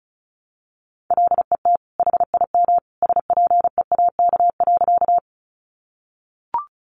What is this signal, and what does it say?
Morse code, heard as a keyed tone, about 34 wpm, 700 Hz. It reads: LET HIM SPEAK.
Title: Sentences from Top 1100 Words 35wpm